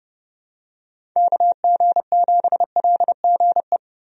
Morse code, heard as a keyed tone, about 32 wpm, 700 Hz. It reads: KG7LGE
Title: USA Call Signs - 6 Characters - Repeated 3x - 30wpm